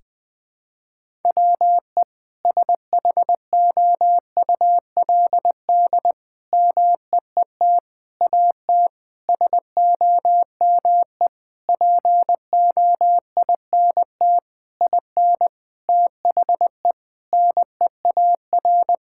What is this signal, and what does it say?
Morse code, heard as a keyed tone, about 21 wpm, 700 Hz. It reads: WE SHOULD MEET AT SOME POINT IN THE NEAR
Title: Sentences with 7-Word Phrases - 20wpm